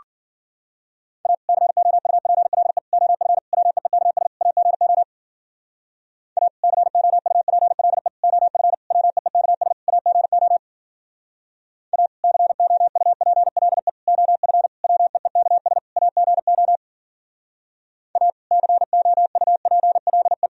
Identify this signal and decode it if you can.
Morse code, heard as a keyed tone, about 52 wpm, 700 Hz. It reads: A COUPLE OF WEEKS AGO A COUPLE OF WEEKS AGO A COUPLE OF WEEKS AGO A COUPLE